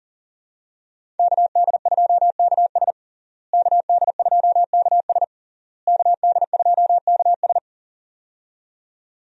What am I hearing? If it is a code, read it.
Morse code, heard as a keyed tone, about 40 wpm, 700 Hz. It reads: KD2KS KD2KS KD2KS
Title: USA Call Signs - 5 Characters - Repeated 3x - 40wpm